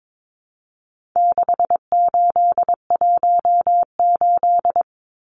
Morse code, heard as a keyed tone, about 22 wpm, 700 Hz. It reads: 6818
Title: Numbers - 4-Digit 22wpm